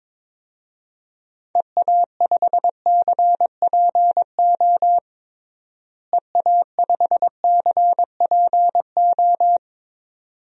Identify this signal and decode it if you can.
Morse code, heard as a keyed tone, about 22 wpm, 700 Hz. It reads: EA5CPO EA5CPO
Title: DXCC-100 Call Signs - 6 Characters - Repeated 2x - 22wpm